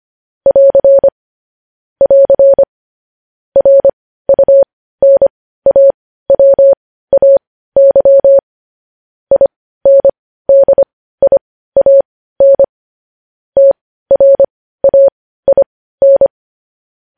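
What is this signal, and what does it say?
Morse code, heard as a keyed tone, about 25 wpm, spaced as for 15 wpm, 550 Hz. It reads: <AR> <AR> RUNAWAY INDIAN TRAIN
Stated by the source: Morse practice sending news headlines at 15 wpm.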